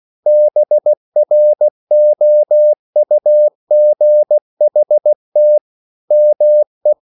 Morse code, heard as a keyed tone, about 16 wpm, 600 Hz. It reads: BROUGHT ME